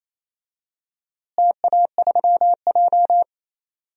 Morse code, heard as a keyed tone, about 28 wpm, 700 Hz. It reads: TA3J
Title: DXCC-100 Call Signs - 4 Characters - Repeated 3x - 28wpm